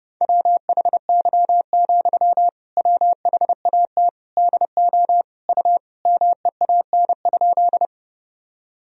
Morse code, heard as a keyed tone, about 30 wpm, 700 Hz. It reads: WHY, WHAT DO U MEAN?
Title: Sentences from Top 200 Words and Abbreviations 30wpm